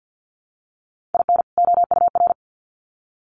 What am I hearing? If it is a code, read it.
Morse code, heard as a keyed tone, about 50 wpm, 700 Hz. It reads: IN OUR
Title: Sets of 2 Words 50wpm